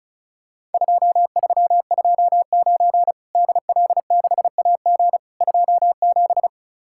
Morse code, heard as a keyed tone, about 36 wpm, 700 Hz. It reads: 2329 DL6AG 27
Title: Worked All Europe DX Contest - QTC Exchange - 35wpm